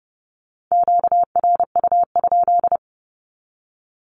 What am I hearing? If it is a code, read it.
Morse code, heard as a keyed tone, about 30 wpm, 700 Hz. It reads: QRU?
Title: Q-Codes with Spoken Meaning - 30wpm